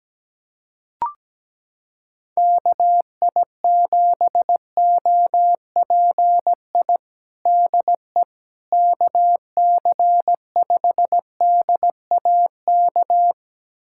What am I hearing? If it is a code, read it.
Morse code, heard as a keyed tone, about 17 wpm, 700 Hz. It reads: KI7OPI DE KC5DAK
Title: USA Callsign DE Callsign 17wpm